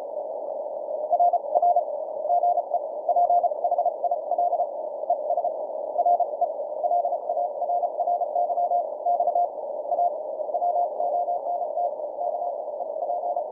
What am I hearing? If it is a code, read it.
Morse code, heard as a keyed tone, about 32 wpm, spaced as for 28 wpm, 700 Hz. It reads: RR GE PHIL ES RE LARRY X A WONT K EP
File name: archived / CWBros